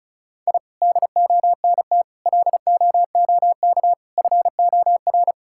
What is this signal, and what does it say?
Morse code, heard as a keyed tone, about 34 wpm, 700 Hz. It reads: I DONT LOOK FOR
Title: Sentences from Top 1000 Words - Review Part 5 - 35wpm